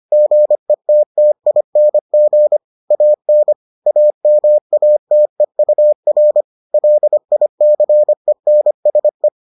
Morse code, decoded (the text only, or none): GETTING AN AMATEUR LICENSE